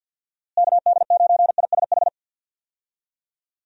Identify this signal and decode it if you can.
Morse code, heard as a keyed tone, about 52 wpm, 700 Hz. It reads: KD9ISH
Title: USA Call Signs - 6 Characters - Repeated 3x - 50wpm